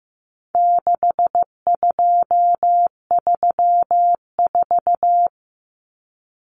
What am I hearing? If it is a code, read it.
Morse code, heard as a keyed tone, about 15 wpm, 700 Hz. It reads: 6234
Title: Common Cut Numbers - 4-Digit 15wpm